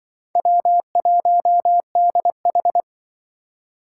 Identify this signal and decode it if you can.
Morse code, heard as a keyed tone, about 24 wpm, 700 Hz. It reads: W1DH
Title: Speed Racing - USA 1x2 Call Signs 20wpm